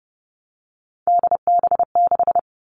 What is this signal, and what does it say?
Morse code, heard as a keyed tone, about 30 wpm, 700 Hz. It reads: DB6
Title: All Cut Numbers - 3-Digit 30wpm